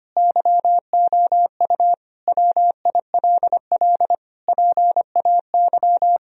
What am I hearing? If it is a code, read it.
Morse code, heard as a keyed tone, about 25 wpm, 700 Hz. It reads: YOU WILL PAY
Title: Sentences from Top 1000 Words - Review Part 4 - 25wpm